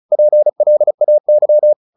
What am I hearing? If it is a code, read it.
Morse code, heard as a keyed tone, about 34 wpm, 600 Hz. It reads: PLAY
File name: the old man and the sea 2 35wpm